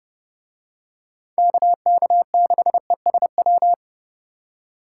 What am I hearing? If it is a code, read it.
Morse code, heard as a keyed tone, about 30 wpm, 700 Hz. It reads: KK6ESW